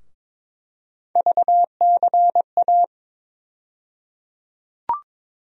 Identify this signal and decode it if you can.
Morse code, heard as a keyed tone, about 22 wpm, 700 Hz. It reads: VCA
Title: USA Call Sign Postfixes - 3 Letters 22wpm